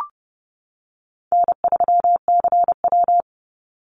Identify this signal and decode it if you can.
Morse code, heard as a keyed tone, about 30 wpm, 700 Hz. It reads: N3CW
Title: Easy Call Signs - 30wpm